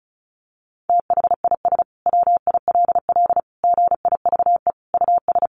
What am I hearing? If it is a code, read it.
Morse code, heard as a keyed tone, about 34 wpm, 700 Hz. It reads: THIS WILL GIVE US
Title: Sentences from Top 1300 Words 35wpm